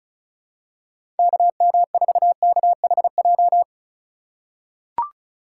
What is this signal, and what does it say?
Morse code, heard as a keyed tone, about 34 wpm, 700 Hz. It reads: KM4KHJ
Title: USA Call Signs - 6 Characters - 3x Character Spacing - 35wpm